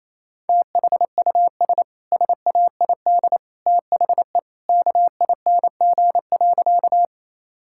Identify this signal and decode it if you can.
Morse code, heard as a keyed tone, about 28 wpm, 700 Hz. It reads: THUS SAID THE KING.
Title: Sentences from Top 1000 Words - Review Part 5 - 28wpm